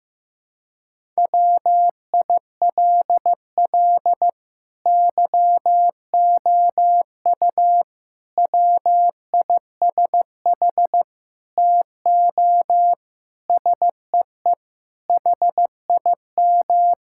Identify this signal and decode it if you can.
Morse code, heard as a keyed tone, about 15 wpm, 700 Hz. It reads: WILL YOU WISH TO SEE HIM
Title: Sentences from Top 1600 Words 15wpm